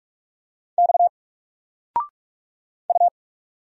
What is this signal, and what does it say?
Morse code, heard as a keyed tone, about 46 wpm, 700 Hz. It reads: X U
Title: Single Letter-Number 45wpm